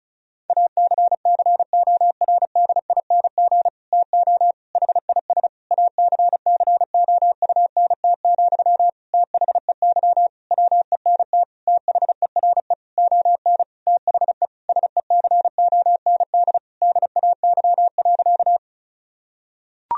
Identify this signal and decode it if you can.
Morse code, heard as a keyed tone, about 34 wpm, 700 Hz. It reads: ACCORDING TO HIS ACCOUNT, THEY WENT THERE ON THE SECOND DAY.
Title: Sentences from Top 1000 Words - Review Part 4 - 35wpm